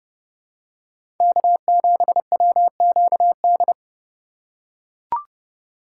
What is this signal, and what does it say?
Morse code, heard as a keyed tone, about 30 wpm, 700 Hz. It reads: K7WQD
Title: USA Call Signs - 6x Character Spacing - 30wpm